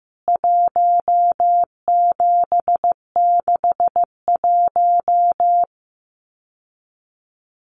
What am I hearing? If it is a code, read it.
Morse code, heard as a keyed tone, about 15 wpm, 700 Hz. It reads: 1761
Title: CWT Exchange 15wpm